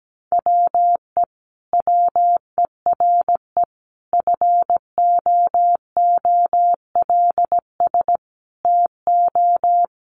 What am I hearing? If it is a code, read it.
Morse code, heard as a keyed tone, about 17 wpm, 700 Hz. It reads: WE WERE FOOLS TO
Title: A Sound Of Thunder 17wpm